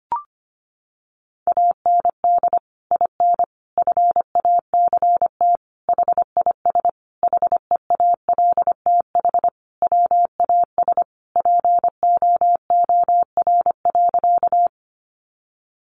Morse code, heard as a keyed tone, about 25 wpm, 700 Hz. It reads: AND IN FACT HIS HEALTH WAS POOR.
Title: Sentences from Top 1000 Words 25wpm